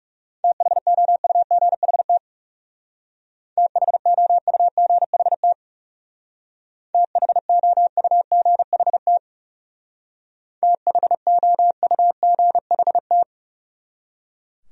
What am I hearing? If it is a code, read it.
Morse code, heard as a keyed tone, about 36 wpm, 700 Hz. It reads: THOUGHT THOUGHT THOUGHT THOUGHT
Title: Speed-Racing Top 300 Words 30wpm